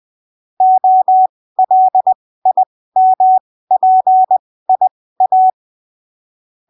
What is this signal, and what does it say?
Morse code, heard as a keyed tone, about 20 wpm, spaced as for 15 wpm, 750 Hz. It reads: OLIMPIA
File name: arquivos / texto / texto 08 20ppm